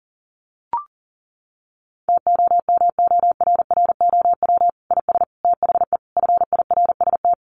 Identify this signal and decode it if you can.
Morse code, heard as a keyed tone, about 40 wpm, 700 Hz. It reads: TOMORROW IS THE FIRST
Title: Additional Easy Phrases 40wpm